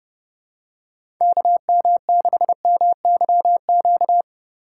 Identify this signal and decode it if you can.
Morse code, heard as a keyed tone, about 30 wpm, 700 Hz. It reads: KM6MYQ